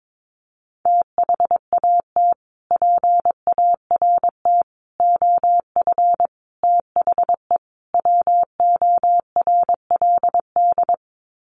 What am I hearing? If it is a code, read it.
Morse code, heard as a keyed tone, about 22 wpm, 700 Hz. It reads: THAT PART OF THE WORLD
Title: Sets of 5 Words Encore2 22wpm